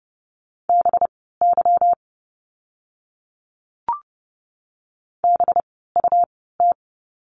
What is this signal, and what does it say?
Morse code, heard as a keyed tone, about 30 wpm, spaced as for 17 wpm, 700 Hz. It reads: BY BUT
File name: course / 30:10wpm / 237 Lesson 38f - Letter q - Words 30wpm